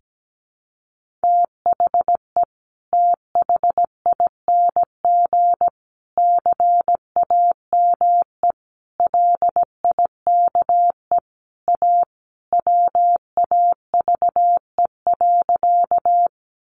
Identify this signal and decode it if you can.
Morse code, heard as a keyed tone, about 17 wpm, 700 Hz. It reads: THE THING CAME LIKE A WAVE.
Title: Sentences from Top 1400 Words 17wpm